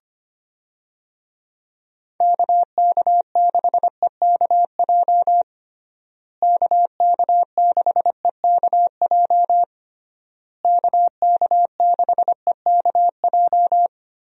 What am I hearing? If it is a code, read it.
Morse code, heard as a keyed tone, about 25 wpm, 700 Hz. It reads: KK6EKJ KK6EKJ KK6EKJ
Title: USA Call Signs - 6 Characters - Repeated 3x - 25wpm